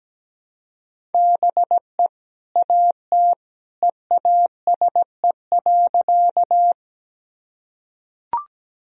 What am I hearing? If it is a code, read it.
Morse code, heard as a keyed tone, about 17 wpm, 700 Hz. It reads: BE AT EASE.
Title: Sentences from Top 1000 Words - Review Part 1 - 17wpm